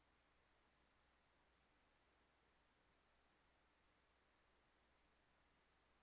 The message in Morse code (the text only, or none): none